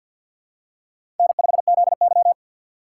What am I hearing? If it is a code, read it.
Morse code, heard as a keyed tone, about 52 wpm, 700 Hz. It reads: N5ZY
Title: USA Call Signs - 4 Characters - 3x Character Spacing - 50wpm